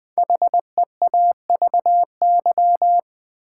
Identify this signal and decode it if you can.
Morse code, heard as a keyed tone, about 20 wpm, 700 Hz. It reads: HEAVY
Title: Sentences from Top 1000 Words - Review Part 6 - New Word Drill - 20wpm